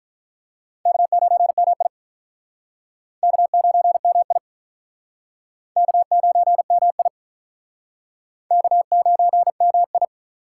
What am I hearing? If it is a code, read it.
Morse code, heard as a keyed tone, about 48 wpm, 700 Hz. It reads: K9MI K9MI K9MI K9MI